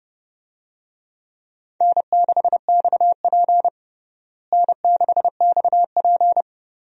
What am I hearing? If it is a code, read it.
Morse code, heard as a keyed tone, about 32 wpm, 700 Hz. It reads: N6XP N6XP
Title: USA Call Signs - 4 Characters - Repeated 2x - 30wpm